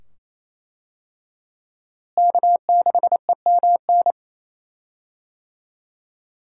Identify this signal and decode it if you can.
Morse code, heard as a keyed tone, about 27 wpm, 700 Hz. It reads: K6EMN
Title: USA Call Signs - 5 Characters - Repeated 2x - 28wpm